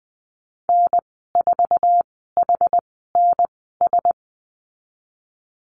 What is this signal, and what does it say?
Morse code, heard as a keyed tone, about 20 wpm, spaced as for 15 wpm, 700 Hz. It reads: N4HNS
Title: 069 Lesson 12g - Letter d - Call Signs 20wpm